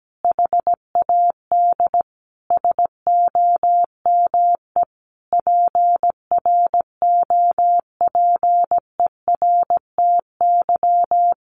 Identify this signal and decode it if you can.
Morse code, heard as a keyed tone, about 17 wpm, 700 Hz. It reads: HAD SOME PROPERTY
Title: Sentences from Top 1500 Words 17wpm